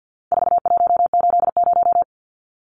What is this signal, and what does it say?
Morse code, heard as a keyed tone, about 50 wpm, 700 Hz. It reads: 4180
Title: Numbers - 4-Digit 50wpm